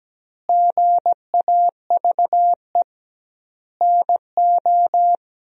Morse code, heard as a keyed tone, about 17 wpm, 700 Hz. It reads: GAVE NO
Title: 2x Word-Spacing - Sentences from Top 500 Words - 17wpm